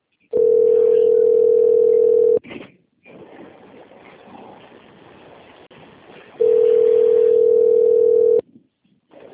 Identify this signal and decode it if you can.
Ringback tone (North America), 440 + 480 Hz.
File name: rg-1600-unknown- - - .23156